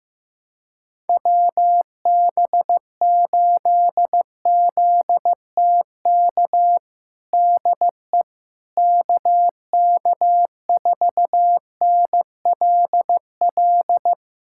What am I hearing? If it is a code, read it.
Morse code, heard as a keyed tone, about 15 wpm, 700 Hz. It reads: WB8ZTK DE KK4NLL
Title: USA Callsign DE Callsign - 6 Characters - 15wpm